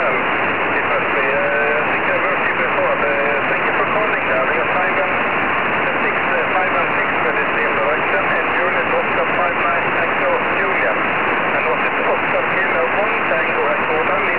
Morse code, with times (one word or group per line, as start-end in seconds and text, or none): none